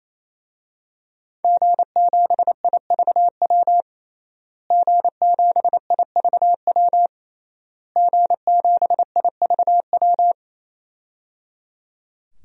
Morse code, heard as a keyed tone, about 28 wpm, 700 Hz. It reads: G7IVW G7IVW G7IVW
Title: DXCC-100 Call Signs - 5 Characters - Repeated 3x - 28wpm